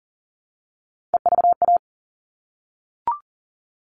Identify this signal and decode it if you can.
Morse code, heard as a keyed tone, about 40 wpm, 700 Hz. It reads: EVA
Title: All Cut Numbers - 3-Digit 40wpm